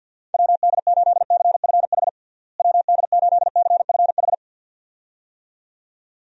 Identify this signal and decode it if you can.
Morse code, heard as a keyed tone, about 49 wpm, 700 Hz. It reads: WD8CFH WD8CFH
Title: USA Call Signs - 6 Characters - Repeated 2x - 50wpm